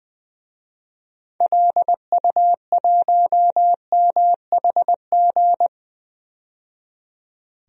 Morse code, heard as a keyed tone, about 20 wpm, 700 Hz. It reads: LU1MHG
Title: DXCC-100 Call Signs - 6 Characters - 20wpm